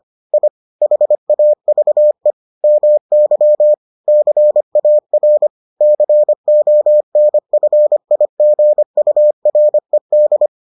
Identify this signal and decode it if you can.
Morse code, heard as a keyed tone, about 25 wpm, 600 Hz. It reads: I HAVE MY CAR CONFIGURED